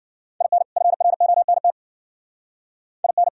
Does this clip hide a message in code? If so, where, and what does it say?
Yes: Morse code from 0.4 to 3.3 s, heard as a keyed tone, about 60 wpm, 700 Hz. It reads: IN FRONT IN